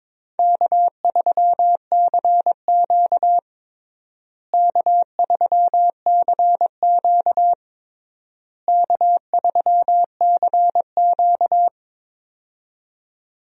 Morse code, heard as a keyed tone, about 22 wpm, 700 Hz. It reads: K3CQ K3CQ K3CQ
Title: USA Call Signs - 4 Characters - Repeated 3x - 22wpm